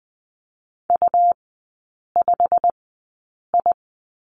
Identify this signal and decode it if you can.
Morse code, heard as a keyed tone, about 20 wpm, spaced as for 8 wpm, 700 Hz. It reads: U5I
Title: 121 Lesson 20e - Number 3 - Three Characters 20wpm